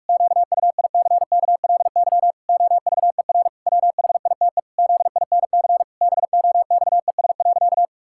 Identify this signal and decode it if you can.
Morse code, heard as a keyed tone, about 46 wpm, 700 Hz. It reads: QUICKLY OVER WHITE ZINC BOXES.
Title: Pangram Challenge v2 - 45wpm